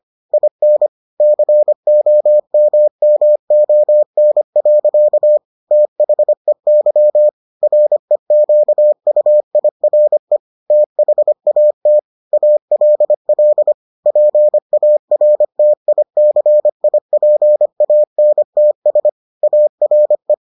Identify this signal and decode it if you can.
Morse code, heard as a keyed tone, about 25 wpm, 600 Hz. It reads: IN COMMON. THEY REQUIRE THAT ALL PARTICIPANTS ARE